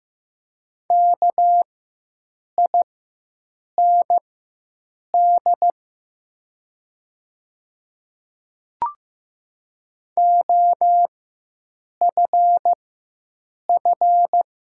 Morse code, heard as a keyed tone, about 15 wpm, spaced as for 7 wpm, 700 Hz. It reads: KIND OFF
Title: 4x Character Spacing - Top 300 Words - 15wpm